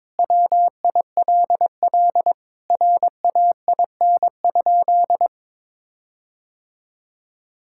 Morse code, heard as a keyed tone, about 22 wpm, 700 Hz. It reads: WILL RAIN?
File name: practice / 22wpm / Sentences with 4-Word Phrases - 22wpm